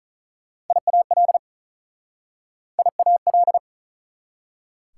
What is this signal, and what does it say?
Morse code, heard as a keyed tone, about 48 wpm, 690 Hz. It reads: IAL IAL